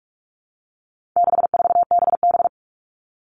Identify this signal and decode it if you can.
Morse code, heard as a keyed tone, about 44 wpm, 700 Hz. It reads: 64BB
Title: All Cut Numbers - 4-Digit 45wpm